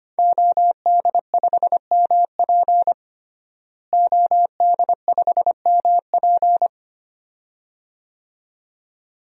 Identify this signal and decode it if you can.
Morse code, heard as a keyed tone, about 25 wpm, 700 Hz. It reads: OD5MP OD5MP
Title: DXCC-100 Call Signs - 5 Characters - Repeated 2x - 25wpm